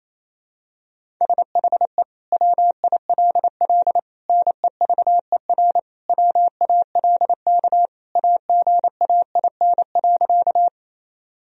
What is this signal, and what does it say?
Morse code, heard as a keyed tone, about 27 wpm, 700 Hz. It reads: SHE WILL NEVER WALK AGAIN.
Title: Sentences from Top 1000 Words - Review Part 1 - 28wpm